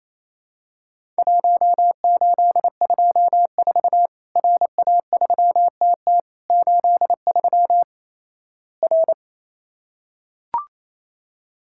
Morse code, heard as a keyed tone, about 27 wpm, 700 Hz. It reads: 1824 RA3TT 83 R
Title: Worked All Europe DX Contest - QTC Exchange - 28wpm